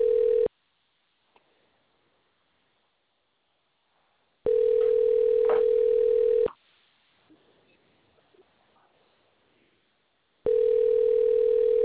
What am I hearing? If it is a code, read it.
Ringback tone (North America), 440 + 480 Hz.